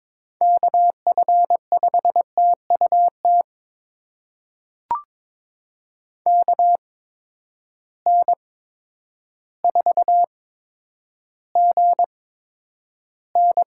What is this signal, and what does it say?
Morse code, heard as a keyed tone, about 22 wpm, spaced as for 10 wpm, 700 Hz. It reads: KF5TUT KN4GN
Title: USA Call Signs - 8x Character Spacing - 22wpm